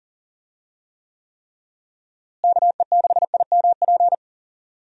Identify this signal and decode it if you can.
Morse code, heard as a keyed tone, about 37 wpm, 700 Hz. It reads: KE6IMP
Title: USA Call Signs - 6 Characters - Repeated 3x - 40wpm